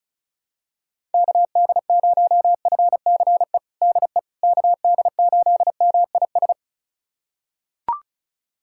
Morse code, heard as a keyed tone, about 36 wpm, 700 Hz. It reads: KD0FCE DE KD8MIS
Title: USA Callsign DE Callsign 35wpm